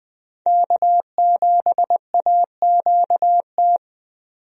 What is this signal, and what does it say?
Morse code, heard as a keyed tone, about 20 wpm, 700 Hz. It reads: K7AQT
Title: USA Call Signs - 5 Characters - 20wpm